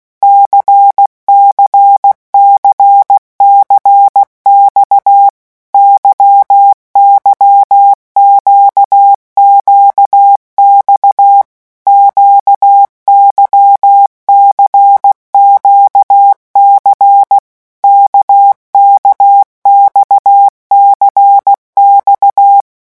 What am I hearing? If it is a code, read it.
Morse code, heard as a keyed tone, about 16 wpm, 800 Hz. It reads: CCCCX YYQQX QYCQC KKXCX
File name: morse mp3 / revision1-vitesse 15 mots